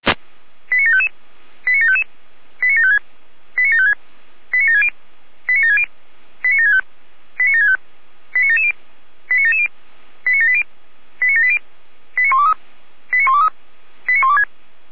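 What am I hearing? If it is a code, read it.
Five-tone selective call (ZVEI), heard as quick runs of five tones, tones of about 70 ms. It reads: 89755 89755 89756 89756 89760 89760 89765 89765 89770 89770 89780 89780 89123 89123 89126
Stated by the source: ZVEI five-tone selective call received by SDR.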